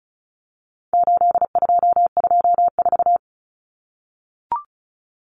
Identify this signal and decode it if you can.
Morse code, heard as a keyed tone, about 34 wpm, 700 Hz. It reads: 8224
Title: Common Cut Numbers - 4-Digit 35wpm